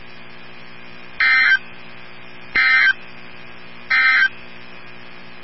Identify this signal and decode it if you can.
Emergency Alert System (SAME) tones, heard as harsh buzzing bursts. It is heard from 1.1 to 4.4 s.
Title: NOAA Weather Radio 162.550